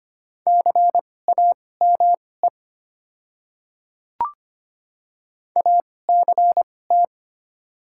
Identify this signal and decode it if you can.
Morse code, heard as a keyed tone, about 25 wpm, spaced as for 18 wpm, 700 Hz. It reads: CAME ACT
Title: 2x Character Spacing - Top 200 Words - 25wpm